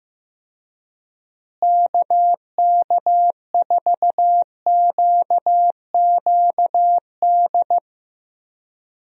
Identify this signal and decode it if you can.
Morse code, heard as a keyed tone, about 16 wpm, 700 Hz. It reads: KK4QQD
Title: USA Call Signs - 6 Characters - Repeated 2x - 15wpm